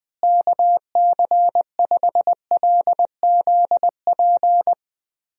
Morse code, heard as a keyed tone, about 21 wpm, 700 Hz. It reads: KC5LZP
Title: USA Call Signs - 6 Characters - Repeated 2x - 20wpm